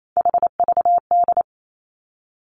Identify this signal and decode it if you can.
Morse code, heard as a keyed tone, about 28 wpm, 700 Hz. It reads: HVD
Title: Three Challenging Letter-Number Combinations 28wpm